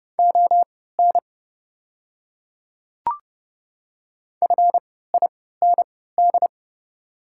Morse code, heard as a keyed tone, about 31 wpm, spaced as for 17 wpm, 690 Hz. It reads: ON FIND
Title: 3x Character Spacing - Top 500 Words - 30wpm